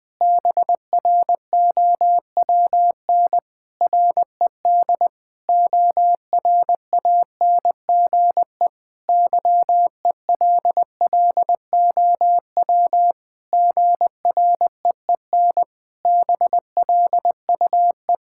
Morse code, heard as a keyed tone, about 21 wpm, 700 Hz. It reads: BROWN RED ORANGE YELLOW GREEN BLUE
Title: Pangram Challenge v2 - 20wpm